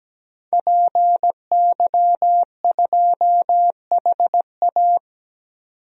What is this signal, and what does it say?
Morse code, heard as a keyed tone, about 17 wpm, 700 Hz. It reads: PY2HA
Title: DXCC-100 Call Signs - 5 Characters - 17wpm